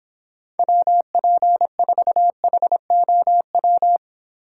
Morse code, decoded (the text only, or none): WP4HOW